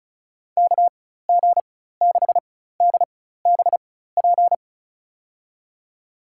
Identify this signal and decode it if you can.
Morse code, heard as a keyed tone, about 36 wpm, spaced as for 16 wpm, 700 Hz. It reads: KG6DBP
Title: USA Call Signs - 6 Characters - 4x Character Spacing - 35wpm